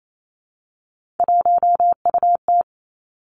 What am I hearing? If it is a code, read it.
Morse code, heard as a keyed tone, about 28 wpm, 700 Hz. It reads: 1UT
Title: Three Challenging Letter-Number Combinations 28wpm